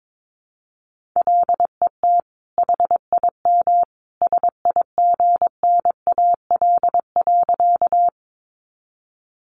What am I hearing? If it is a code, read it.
Morse code, heard as a keyed tone, about 22 wpm, 700 Hz. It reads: LET HIM SIGNAL.
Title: Sentences from Top 1400 Words 22wpm